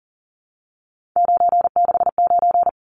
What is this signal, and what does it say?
Morse code, heard as a keyed tone, about 40 wpm, 700 Hz. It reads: 969
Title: Numbers - 3-Digit 40wpm